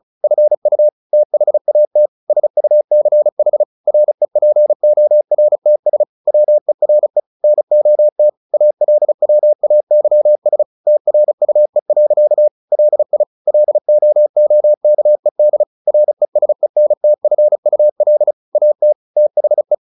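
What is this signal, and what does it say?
Morse code, heard as a keyed tone, about 36 wpm, 600 Hz. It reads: FU THAT SUCH REPORTS WERE NOT ALWAYS TRUE. LI LOOKED RESENTFUL AT THE
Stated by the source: Morse practice at 35 wpm from CW med Gnister, sending Ken Follett's novel Never in English.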